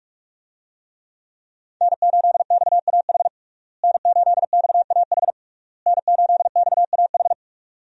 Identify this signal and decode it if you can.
Morse code, heard as a keyed tone, about 46 wpm, 700 Hz. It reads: N8XAH N8XAH N8XAH
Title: USA Call Signs - 5 Characters - Repeated 3x - 45wpm